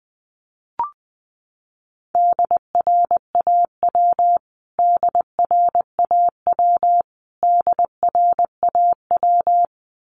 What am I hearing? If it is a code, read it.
Morse code, heard as a keyed tone, about 20 wpm, 700 Hz. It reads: DRAW DRAW DRAW
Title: Top 300 Words Repeated 20wpm